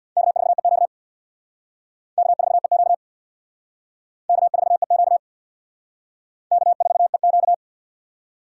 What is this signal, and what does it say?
Morse code, heard as a keyed tone, about 60 wpm, 700 Hz. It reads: K4EQ K4EQ K4EQ K4EQ
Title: Speed Racing - USA Call Signs 50wpm